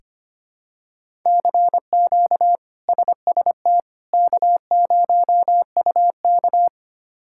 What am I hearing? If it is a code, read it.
Morse code, heard as a keyed tone, about 25 wpm, 700 Hz. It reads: CQ SST K0UK
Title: K1USN SST QSO Practice - 25wpm